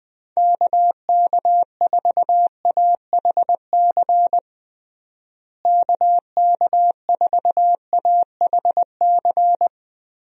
Morse code, heard as a keyed tone, about 20 wpm, 700 Hz. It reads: KK4AHC KK4AHC
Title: USA Call Signs - 6 Characters - Repeated 3x - 20wpm